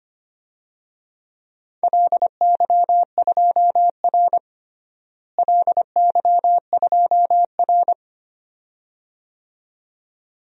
Morse code, heard as a keyed tone, about 25 wpm, 700 Hz. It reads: LY2R LY2R
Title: DXCC-100 Call Signs - 4 Characters - Repeated 2x - 25wpm